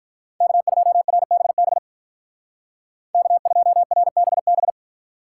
Morse code, heard as a keyed tone, about 44 wpm, 700 Hz. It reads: K2RBB K2RBB